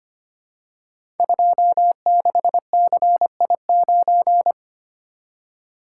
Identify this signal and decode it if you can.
Morse code, heard as a keyed tone, about 25 wpm, 700 Hz. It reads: 26CI9
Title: Code Groups - 5 Letter:Numbers - 25wpm